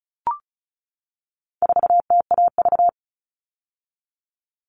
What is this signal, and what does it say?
Morse code, heard as a keyed tone, about 34 wpm, 700 Hz. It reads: 4TAV
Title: All Cut Numbers - 4-Digit 35wpm